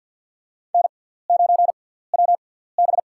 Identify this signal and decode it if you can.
Morse code, heard as a keyed tone, about 48 wpm, spaced as for 17 wpm, 690 Hz. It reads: N9WB
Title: USA Call Signs - 4 Characters - 6x Character Spacing - 50wpm